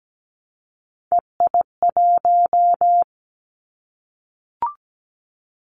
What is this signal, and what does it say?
Morse code, heard as a keyed tone, about 17 wpm, 700 Hz. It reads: EI1
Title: Callsign Prefixes for Top 50 DXCC - 17wpm